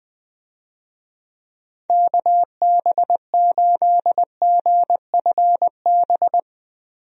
Morse code, heard as a keyed tone, about 20 wpm, 700 Hz. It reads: KB8GFB
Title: USA Call Signs - 6 Characters - Repeated 2x - 20wpm